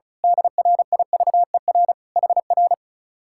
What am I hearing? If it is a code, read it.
Morse code, heard as a keyed tone, about 32 wpm, 700 Hz. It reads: DRIVER HR
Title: Speed-Racing Common QSO Phrases Encore - 35wpm